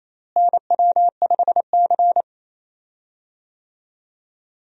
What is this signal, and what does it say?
Morse code, heard as a keyed tone, about 28 wpm, 700 Hz. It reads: NW5C
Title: USA Call Signs - 4 Characters - Repeated 3x - 28wpm